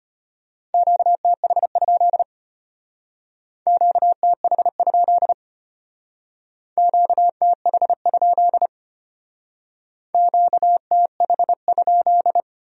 Morse code, heard as a keyed tone, about 32 wpm, 700 Hz. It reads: QTH? QTH? QTH? QTH?